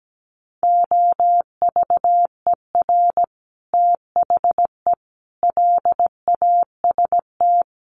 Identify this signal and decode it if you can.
Morse code, heard as a keyed tone, about 17 wpm, 700 Hz. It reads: OVER THE LAST
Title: Sets of 5 Words Encore 17wpm